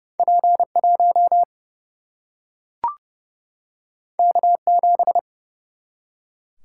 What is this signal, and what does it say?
Morse code, heard as a keyed tone, about 30 wpm, 700 Hz. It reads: P1 K7
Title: Single-Letter Single-Number 30wpm